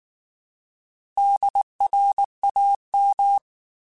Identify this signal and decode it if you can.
Morse code, heard as a keyed tone, about 19 wpm, 800 Hz. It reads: DRAM
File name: Alphabetical / Alphabetical mp3s / Ds @18